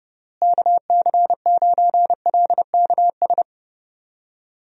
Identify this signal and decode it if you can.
Morse code, heard as a keyed tone, about 30 wpm, 700 Hz. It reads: KC9LKS
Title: USA Call Signs - 6 Characters - Repeated 3x - 30wpm